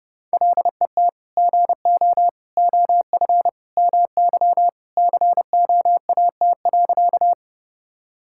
Morse code, heard as a keyed tone, about 29 wpm, 700 Hz. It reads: LET GO OF MY COAT.
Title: Sentences from Top 1600 Words 30wpm